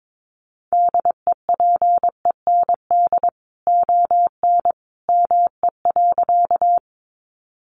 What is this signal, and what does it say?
Morse code, heard as a keyed tone, about 22 wpm, 700 Hz. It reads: DEPEND ON ME.
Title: Sentences from Top 1300 Words 22wpm